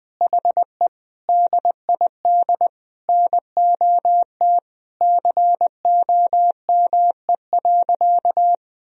Morse code, heard as a keyed tone, about 20 wpm, 700 Hz. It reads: HE DID NOT COME.
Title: Sentences from Top 1000 Words - Review Part 4 - 20wpm